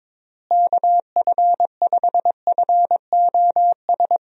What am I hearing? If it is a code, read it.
Morse code, heard as a keyed tone, about 22 wpm, 700 Hz. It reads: KF5FOS
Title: USA Call Signs - 6 Characters - Repeated 3x - 22wpm